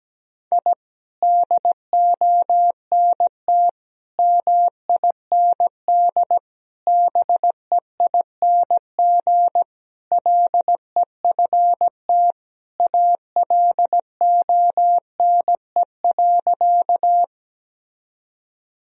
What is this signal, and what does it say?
Morse code, heard as a keyed tone, about 17 wpm, 700 Hz. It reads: I DONT MIND BEING LEFT ALONE.
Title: Sentences from Top 1000 Words - Review Part 3 - 17wpm